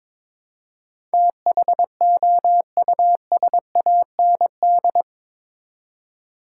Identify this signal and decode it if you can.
Morse code, heard as a keyed tone, about 22 wpm, 700 Hz. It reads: THOUSAND
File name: practice / 22wpm / Eight Letter Words - 22wpm